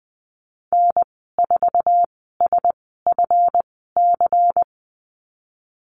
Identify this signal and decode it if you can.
Morse code, heard as a keyed tone, about 20 wpm, spaced as for 15 wpm, 700 Hz. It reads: N4SFC
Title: 147 Lesson 24g - Letter y - Call Signs 20wpm